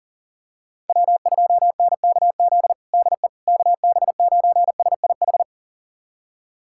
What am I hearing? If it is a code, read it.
Morse code, heard as a keyed tone, about 40 wpm, 700 Hz. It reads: W2NKZ DE KB9SIH
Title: USA Callsign DE Callsign 40wpm